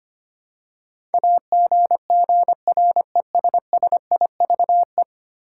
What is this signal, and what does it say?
Morse code, heard as a keyed tone, about 25 wpm, 700 Hz. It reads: AGGRESSIVE
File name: Ten Letter Words - 25wpm